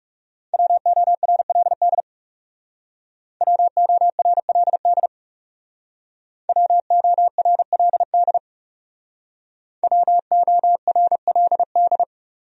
Morse code, heard as a keyed tone, about 37 wpm, 700 Hz. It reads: WORLD WORLD WORLD WORLD
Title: Speed-Racing Top 300 Words 30wpm